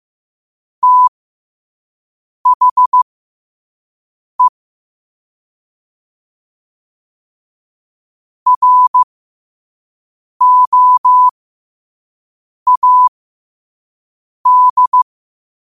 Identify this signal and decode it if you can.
Morse code, heard as a keyed tone, about 15 wpm, spaced as for 5 wpm, 1000 Hz. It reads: THE ROAD
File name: qotdfiles / 5wpm